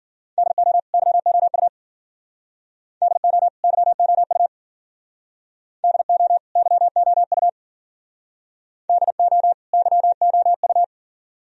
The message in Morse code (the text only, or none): DO YOU DO YOU DO YOU DO YOU